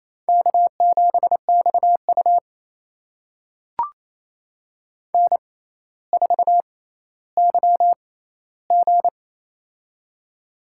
Morse code, heard as a keyed tone, about 28 wpm, spaced as for 14 wpm, 700 Hz. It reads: K7XU N4YG
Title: USA Call Signs - 4 Characters - 6x Character Spacing - 28wpm